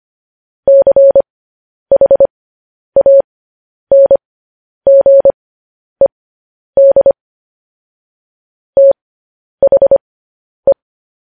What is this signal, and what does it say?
Morse code, heard as a keyed tone, about 25 wpm, spaced as for 10 wpm, 550 Hz. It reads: CHANGED THE